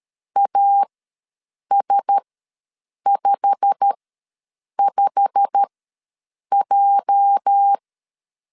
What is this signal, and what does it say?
Morse code, heard as a keyed tone, about 13 wpm, spaced as for 7 wpm, 800 Hz. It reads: AS55J
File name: LEZIONI / 8 YJ45 / 8 YJ45 ws 12wpm 3146